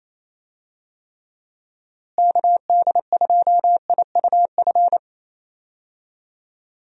Morse code, heard as a keyed tone, about 27 wpm, 700 Hz. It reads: KD2IUF
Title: USA Call Signs - 6 Characters - Repeated 2x - 28wpm